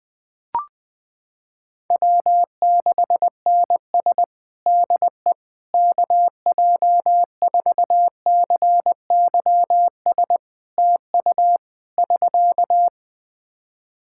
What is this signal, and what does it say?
Morse code, heard as a keyed tone, about 21 wpm, 700 Hz. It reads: W6NS DE KJ4CYS TU <SK>
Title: Practice QSO - Exchange 4 - Ending - 20wpm